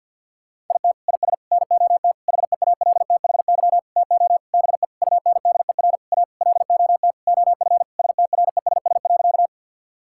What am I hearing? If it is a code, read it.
Morse code, heard as a keyed tone, about 50 wpm, 700 Hz. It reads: IT IS NOT HEALTHY TO BE UNDER A LOT OF STRESS.
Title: Sentences from Top 1000 Words - Review Part 6 - 50wpm